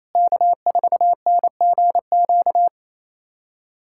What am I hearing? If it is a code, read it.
Morse code, heard as a keyed tone, about 28 wpm, 700 Hz. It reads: K4NGQ
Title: USA Callsign DE Callsign - 5 Characters - 28wpm